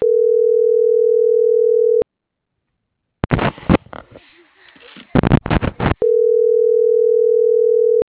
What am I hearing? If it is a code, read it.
Ringback tone (North America), 440 + 480 Hz.